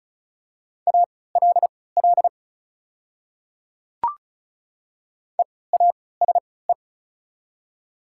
Morse code, heard as a keyed tone, about 36 wpm, spaced as for 20 wpm, 690 Hz. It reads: ALL EASE
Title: 3x Character Spacing - Top 300 Words - 35wpm